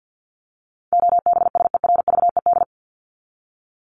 Morse code, heard as a keyed tone, about 48 wpm, 700 Hz. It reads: OBSERVED
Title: ED Suffix - Top 100 Words with Base Words 50wpm